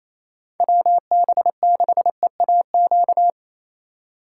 Morse code, heard as a keyed tone, about 34 wpm, 700 Hz. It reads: WB6EAQ
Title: Speed Racing - USA 2x3 Call Signs 28wpm